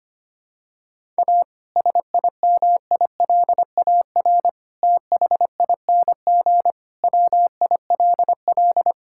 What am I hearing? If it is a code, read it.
Morse code, heard as a keyed tone, about 25 wpm, 700 Hz. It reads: A SIMILAR THING WILL
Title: Sentences from Top 1000 Words - Review Part 5 - 25wpm